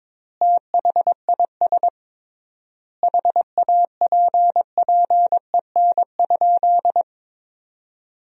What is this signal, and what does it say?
Morse code, heard as a keyed tone, about 22 wpm, 700 Hz. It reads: THIS HAPPEN?
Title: 3x Word-Spacing - Sentences from Top 400 Words - 22wpm